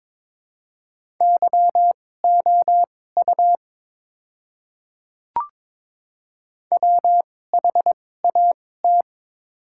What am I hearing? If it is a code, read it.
Morse code, heard as a keyed tone, about 22 wpm, spaced as for 16 wpm, 700 Hz. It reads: YOU WHAT
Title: 2x Character Spacing - Top 100 Words - 22wpm